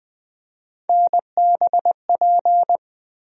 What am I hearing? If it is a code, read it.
Morse code, heard as a keyed tone, about 20 wpm, 700 Hz. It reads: NBP
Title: USA Call Sign Postfixes - 3 Letters 20wpm